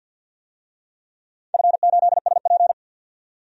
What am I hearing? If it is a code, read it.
Morse code, heard as a keyed tone, about 51 wpm, 700 Hz. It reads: F8SP
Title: DXCC-100 Call Signs - 4 Characters - 50wpm